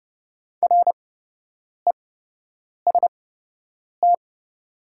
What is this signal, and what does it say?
Morse code, heard as a keyed tone, about 32 wpm, spaced as for 8 wpm, 690 Hz. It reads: REST
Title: 8x Character Spacing - Top 500 Words - 30wpm